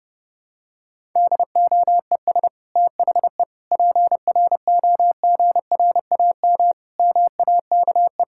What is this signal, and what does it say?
Morse code, heard as a keyed tone, about 32 wpm, 700 Hz. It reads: DOES THE PROGRAM MAKE
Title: Sentences from Top 1000 Words - Review Part 5 - 30wpm